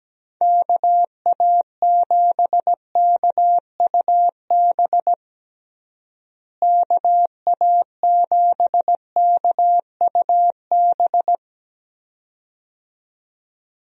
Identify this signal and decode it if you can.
Morse code, heard as a keyed tone, about 17 wpm, 700 Hz. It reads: KA7KUB KA7KUB
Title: USA Call Signs - 6 Characters - Repeated 2x - 17wpm